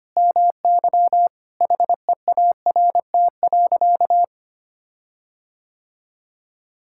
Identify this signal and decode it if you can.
Morse code, heard as a keyed tone, about 25 wpm, 700 Hz. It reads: MY HEART.
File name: practice / 25wpm / Sentences with 9-Word Phrases - 25wpm